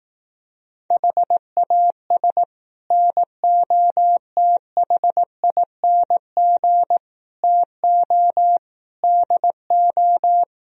Morse code, heard as a keyed tone, about 18 wpm, 700 Hz. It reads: HAS NOTHING TO DO
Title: Speed-Racing - Sets of 5 Words 15wpm